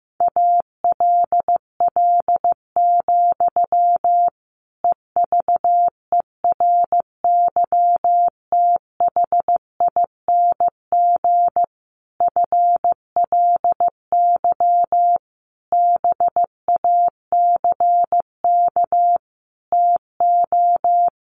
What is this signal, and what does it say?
Morse code, heard as a keyed tone, about 15 wpm, 700 Hz. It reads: ALL, EVERYTHING FLY BACK TO
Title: A Sound Of Thunder 15wpm